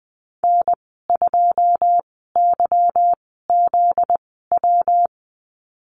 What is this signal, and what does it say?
Morse code, heard as a keyed tone, about 20 wpm, spaced as for 14 wpm, 700 Hz. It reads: N2YZW
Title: 245 Lesson 39g - Letter z - Call Signs 20wpm